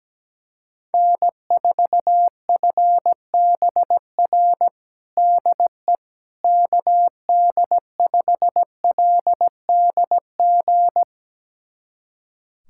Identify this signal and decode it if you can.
Morse code, heard as a keyed tone, about 17 wpm, 700 Hz. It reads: N4FBR DE KD5LDG
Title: USA Callsign DE Callsign 17wpm